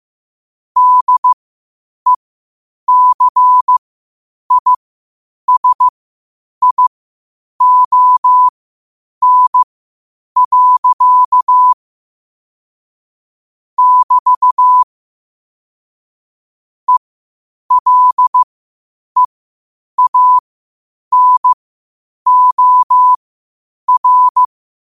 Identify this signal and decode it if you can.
Morse code, heard as a keyed tone, about 15 wpm, spaced as for 8 wpm, 1000 Hz. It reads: DECISION. <BT> ELEANOR
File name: qotdfiles / 8wpm